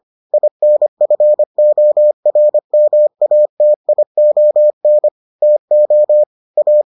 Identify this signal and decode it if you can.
Morse code, heard as a keyed tone, about 25 wpm, 600 Hz. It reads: INFORMATION TO A